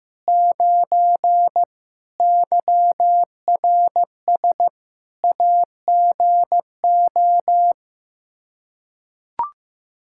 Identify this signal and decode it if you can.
Morse code, heard as a keyed tone, about 15 wpm, 700 Hz. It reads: 9 YRS AGO
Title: Practice QSO - Exchange 3 - More Info - 15wpm